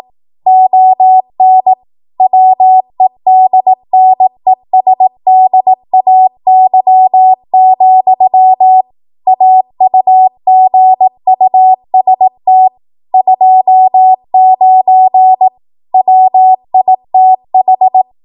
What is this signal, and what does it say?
Morse code, heard as a keyed tone, about 18 wpm, 750 Hz. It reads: ON WEDNESDAY, AUGUST 29 WITH
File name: Archive / 18 WPM / 18WPM